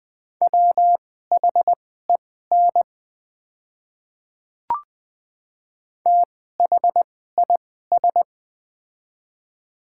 Morse code, heard as a keyed tone, about 20 wpm, spaced as for 14 wpm, 700 Hz. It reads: WHEN THIS
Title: 2x Character Spacing - Top 200 Words - 20wpm